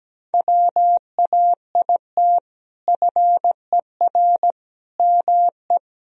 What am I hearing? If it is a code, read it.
Morse code, heard as a keyed tone, about 17 wpm, 700 Hz. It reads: WAIT FER ME
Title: Sentences from Top 500 Words and Abbreviations 17wpm